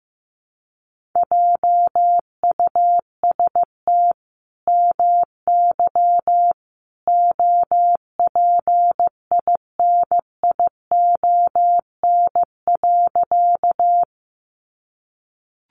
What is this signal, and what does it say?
Morse code, heard as a keyed tone, about 15 wpm, 700 Hz. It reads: JUST MY OPINION.
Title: Sentences from Top 1200 Words 15wpm